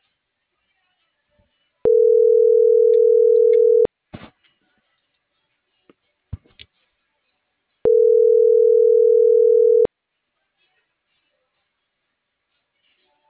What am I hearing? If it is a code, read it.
Ringback tone (North America), 440 + 480 Hz.